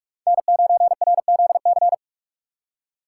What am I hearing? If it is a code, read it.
Morse code, heard as a keyed tone, about 46 wpm, 700 Hz. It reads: N9RZC
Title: USA Call Signs - 5 Characters - 3x Character Spacing - 45wpm